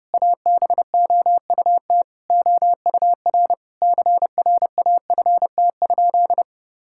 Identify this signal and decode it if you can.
Morse code, heard as a keyed tone, about 30 wpm, 700 Hz. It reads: ABOUT OUR CRAFT?
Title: Sentences from Top 1000 Words - Review Part 6 - 30wpm